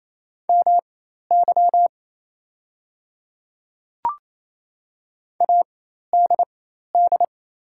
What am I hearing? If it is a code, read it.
Morse code, heard as a keyed tone, about 27 wpm, spaced as for 13 wpm, 700 Hz. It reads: MY ADD
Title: 4x Character Spacing - Top 300 Words - 28wpm